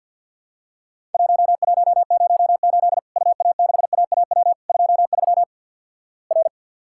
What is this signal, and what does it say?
Morse code, heard as a keyed tone, about 50 wpm, 700 Hz. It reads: 1108 UA6AAW 23 R
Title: Worked All Europe DX Contest - QTC Exchange - 50wpm